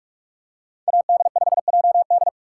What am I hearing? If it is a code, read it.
Morse code, heard as a keyed tone, about 41 wpm, 700 Hz. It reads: AD5JD
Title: USA Call Signs - 5 Characters - 4x Character Spacing - 45wpm